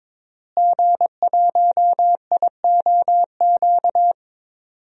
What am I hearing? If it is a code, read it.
Morse code, heard as a keyed tone, about 21 wpm, 700 Hz. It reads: G1IOQ
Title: DXCC-100 Call Signs - 5 Characters - Repeated 3x - 22wpm